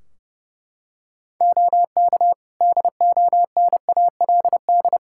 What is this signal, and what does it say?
Morse code, heard as a keyed tone, about 30 wpm, 700 Hz. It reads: OK DONALD
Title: Practice QSO - Exchange 3 - More Info - 30wpm